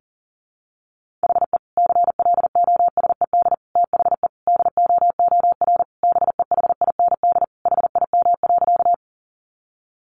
Morse code, heard as a keyed tone, about 40 wpm, 700 Hz. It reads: HE CLOSED THE DOOR BEHIND HIM.
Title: Sentences from Top 1400 Words 40wpm